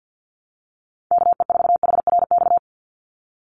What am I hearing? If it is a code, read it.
Morse code, heard as a keyed tone, about 50 wpm, 700 Hz. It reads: KE4HRX
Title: USA Call Signs 50wpm